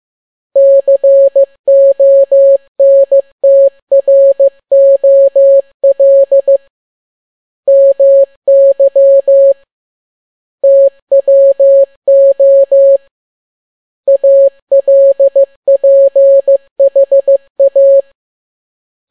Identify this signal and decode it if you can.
Morse code, heard as a keyed tone, about 15 wpm, 550 Hz. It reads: CONTROL MY TWO ALPHA